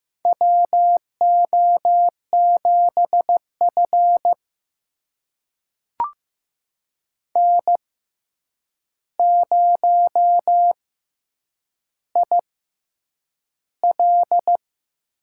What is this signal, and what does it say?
Morse code, heard as a keyed tone, about 15 wpm, spaced as for 8 wpm, 700 Hz. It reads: WO7F N0IL
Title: USA Call Signs - 6x Character Spacing - 15wpm